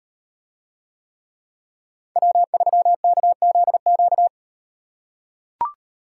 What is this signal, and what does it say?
Morse code, heard as a keyed tone, about 38 wpm, 700 Hz. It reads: W3KZQ